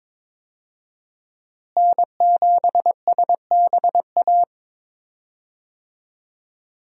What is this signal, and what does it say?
Morse code, heard as a keyed tone, about 22 wpm, 700 Hz. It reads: N7SBA
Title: USA Call Signs - 5 Characters - Repeated 3x - 22wpm